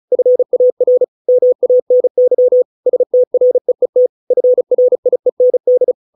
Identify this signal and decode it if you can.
Morse code, heard as a keyed tone, about 34 wpm, 500 Hz. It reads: FAR MANY STREET FRIEND